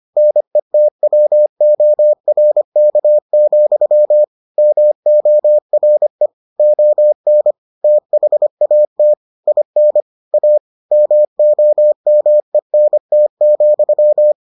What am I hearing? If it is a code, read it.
Morse code, heard as a keyed tone, about 25 wpm, 600 Hz. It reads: NETWORK, MORE ON THAT IN A MOMENT,